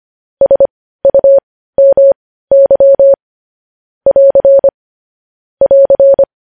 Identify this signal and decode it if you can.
Morse code, heard as a keyed tone, about 25 wpm, spaced as for 15 wpm, 550 Hz. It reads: SUMY <AR> <AR>
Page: News Headlines in Morse Code at 15 WPM